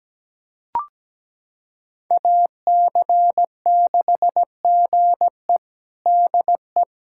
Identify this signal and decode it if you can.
Morse code, heard as a keyed tone, about 17 wpm, 700 Hz. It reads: AC6GE DE
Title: USA Callsign DE Callsign - 5 Characters - 17wpm